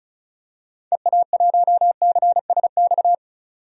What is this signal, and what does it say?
Morse code, heard as a keyed tone, about 36 wpm, 700 Hz. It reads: EA1CSX
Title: DXCC-100 Call Signs - 6 Characters - 35wpm